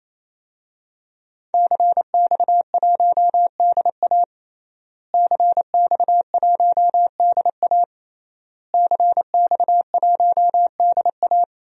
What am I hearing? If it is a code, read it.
Morse code, heard as a keyed tone, about 28 wpm, 700 Hz. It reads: CX1DA CX1DA CX1DA
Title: DXCC-100 Call Signs - 5 Characters - Repeated 3x - 28wpm